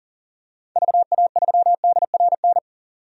Morse code, heard as a keyed tone, about 40 wpm, 700 Hz. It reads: VA3DRN
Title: DXCC-100 Call Signs - 6 Characters - 40wpm